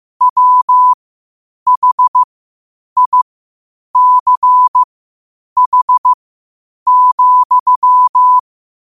Morse code, heard as a keyed tone, about 15 wpm, spaced as for 8 wpm, 1000 Hz. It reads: WHICH,